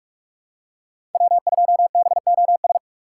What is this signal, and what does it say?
Morse code, heard as a keyed tone, about 44 wpm, 700 Hz. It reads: W2BOS